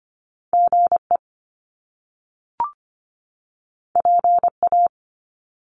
Morse code, heard as a keyed tone, about 25 wpm, 700 Hz. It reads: GE PA
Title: Top Two Letter Combinations in Words 25wpm